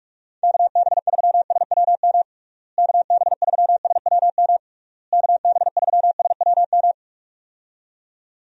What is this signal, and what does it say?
Morse code, heard as a keyed tone, about 46 wpm, 700 Hz. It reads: KB3SWM KB3SWM KB3SWM
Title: USA Call Signs - 6 Characters - Repeated 3x - 45wpm